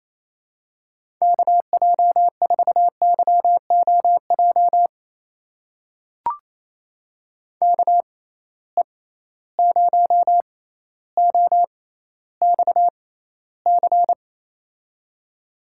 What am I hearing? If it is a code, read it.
Morse code, heard as a keyed tone, about 28 wpm, spaced as for 14 wpm, 700 Hz. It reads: KJ4YOJ KE0OXC
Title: USA Call Signs - 6 Characters - 6x Character Spacing - 28wpm